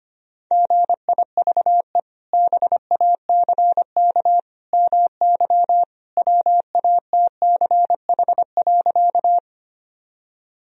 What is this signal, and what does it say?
Morse code, heard as a keyed tone, about 25 wpm, 700 Hz. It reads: GIVE BACK MY WATCH.
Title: Sentences from Top 1000 Words - Review Part 1 - 25wpm